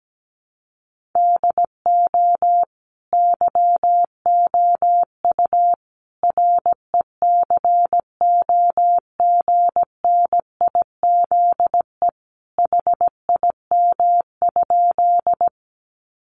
Morse code, heard as a keyed tone, about 17 wpm, 700 Hz. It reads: DO YOU RECOGNIZE HIM?
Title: Sentences from Top 1400 Words 17wpm